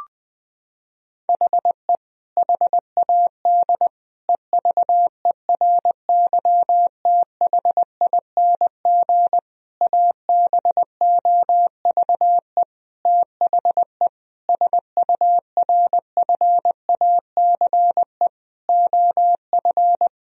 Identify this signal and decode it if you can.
Morse code, heard as a keyed tone, about 20 wpm, 700 Hz. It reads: HE HAD EVERYTHING ABOVE THE SURFACE OF
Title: Sentences from Top 1000 Words - Review Part 3 - 20wpm